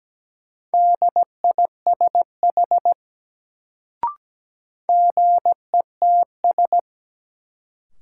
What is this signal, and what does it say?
Morse code, heard as a keyed tone, about 17 wpm, 700 Hz. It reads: DISH GETS
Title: Sentences from Top 1300 Words - New Word Drill Alt - 17wpm